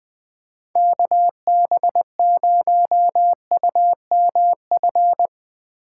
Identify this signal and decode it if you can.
Morse code, heard as a keyed tone, about 20 wpm, 700 Hz. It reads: KB0UMF
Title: USA Call Signs - 6 Characters - 8x Character Spacing - 20wpm